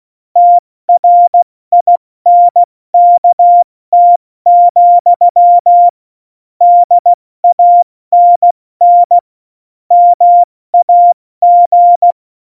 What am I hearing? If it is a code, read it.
Morse code, heard as a keyed tone, about 16 wpm, 700 Hz. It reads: TRINKT, DANN MAG
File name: Morsekurs / Witze / Witz 10.Dezember 2019, 70 F80